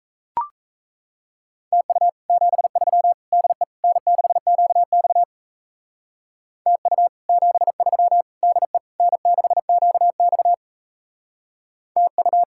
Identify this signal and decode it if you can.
Morse code, heard as a keyed tone, about 33 wpm, 700 Hz. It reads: TU 73 DE N6QX TU 73 DE N6QX TU
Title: Speed-Racing Common QSO Phrases - 28wpm